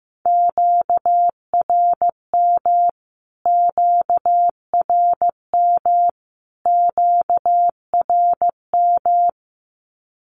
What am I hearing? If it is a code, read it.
Morse code, heard as a keyed tone, about 15 wpm, 700 Hz. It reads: QRM QRM QRM
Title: Q-Codes Repeated 3x 15wpm